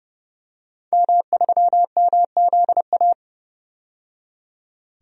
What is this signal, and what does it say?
Morse code, heard as a keyed tone, about 29 wpm, 700 Hz. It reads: M3MZA
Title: DXCC-100 Call Signs - 5 Characters - 30wpm